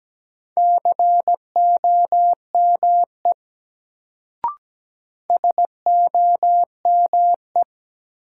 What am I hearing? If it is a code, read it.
Morse code, heard as a keyed tone, about 17 wpm, 700 Hz. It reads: COME SOME
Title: Sentences from Top 100 Words - New Word Drill - 17wpm